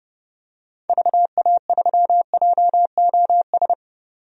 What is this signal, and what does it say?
Morse code, heard as a keyed tone, about 30 wpm, 700 Hz. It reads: VA3JOS
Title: DXCC-100 Call Signs - 6 Characters - 30wpm